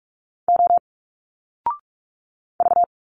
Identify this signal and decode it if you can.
Morse code, heard as a keyed tone, about 42 wpm, 700 Hz. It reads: O V
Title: Single Letters 45wpm